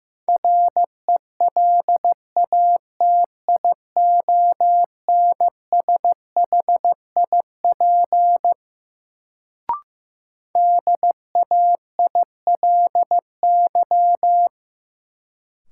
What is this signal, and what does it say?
Morse code, heard as a keyed tone, about 15 wpm, 700 Hz. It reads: RELATIONSHIP DAILY
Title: Sentences from Top 1100 Words - New Word Drill - 15wpm